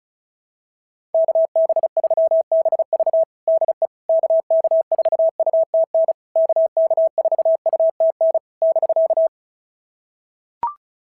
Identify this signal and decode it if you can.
Morse code, heard as a keyed tone, about 33 wpm, 650 Hz. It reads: KB3BV DE KK4UTN KK4UTN <BK>
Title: SOTA QSO Practice 35wpm